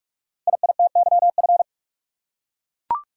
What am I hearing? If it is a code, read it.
Morse code, heard as a keyed tone, about 44 wpm, 700 Hz. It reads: IITYF
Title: Code Groups - 5 Letters - 45wpm